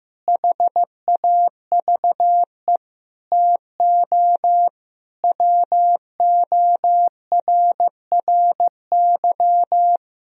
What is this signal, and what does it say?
Morse code, heard as a keyed tone, about 15 wpm, 700 Hz. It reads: HAVE TO WORRY
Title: Speed-Racing - Sets of 5 Words 15wpm